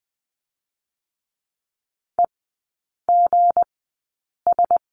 Morse code, heard as a keyed tone, about 20 wpm, spaced as for 8 wpm, 690 Hz. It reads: EGS
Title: 172 Lesson 28e - Number 9 - Three Characters 20wpm